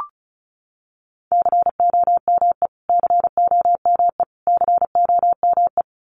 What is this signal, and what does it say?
Morse code, heard as a keyed tone, about 34 wpm, 700 Hz. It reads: COME COME COME
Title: Top 500 Words Repeated 35wpm